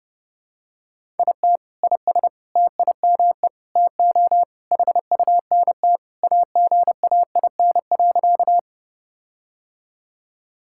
Morse code, heard as a keyed tone, about 30 wpm, 700 Hz. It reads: IT IS TIME TO HUNT AGAIN.
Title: Sentences from Top 1600 Words 30wpm